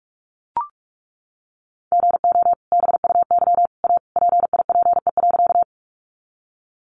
Morse code, heard as a keyed tone, about 44 wpm, 700 Hz. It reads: GO BUY A PIPE.
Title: Sentences from Top 1500 Words 45wpm